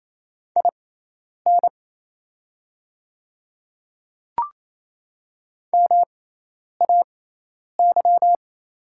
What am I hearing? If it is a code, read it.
Morse code, heard as a keyed tone, about 28 wpm, spaced as for 10 wpm, 690 Hz. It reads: IN MAY
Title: 6x Character Spacing - Top 100 Words - 28wpm